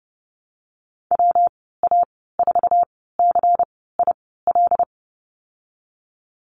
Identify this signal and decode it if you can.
Morse code, heard as a keyed tone, about 30 wpm, spaced as for 17 wpm, 700 Hz. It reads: WA4CIL
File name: Lesson 19g - Letter w - Call Signs 30wpm